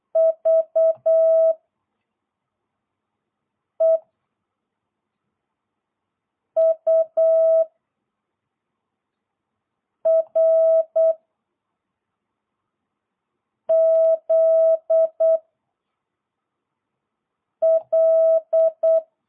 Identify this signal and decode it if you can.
Morse code, heard as a keyed tone, about 8 wpm, 650 Hz. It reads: V E U R Z L